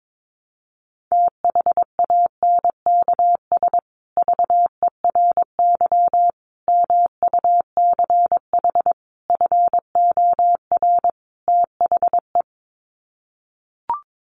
Morse code, heard as a keyed tone, about 22 wpm, 700 Hz. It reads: THANKS VERY MUCH FOR THE
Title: Sets of 5 Words Encore2 22wpm